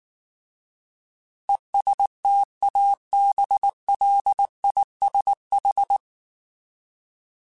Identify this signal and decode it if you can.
Morse code, heard as a keyed tone, about 19 wpm, 800 Hz. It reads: ESTABLISH
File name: Alphabetical / Alphabetical mp3s / Es @18